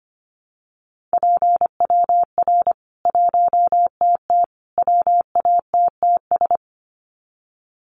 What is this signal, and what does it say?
Morse code, heard as a keyed tone, about 25 wpm, 700 Hz. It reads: PWR 1TT WATTS
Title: Common QSO Phrases 25wpm